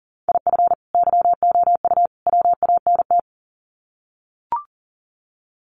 Morse code, heard as a keyed tone, about 40 wpm, 700 Hz. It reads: IF YOU WANT
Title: Sets of 3 Words 40wpm